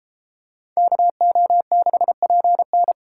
Morse code, heard as a keyed tone, about 32 wpm, 700 Hz. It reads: KO6PN